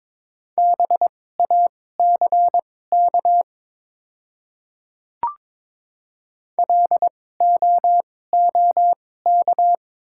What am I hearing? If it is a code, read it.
Morse code, heard as a keyed tone, about 22 wpm, spaced as for 16 wpm, 690 Hz. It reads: BACK LOOK
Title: 2x Character Spacing - Top 500 Words - 22wpm